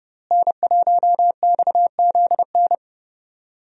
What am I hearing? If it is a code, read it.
Morse code, heard as a keyed tone, about 30 wpm, 700 Hz. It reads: N1XZN
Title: USA Call Signs - 5 Characters - Repeated 2x - 30wpm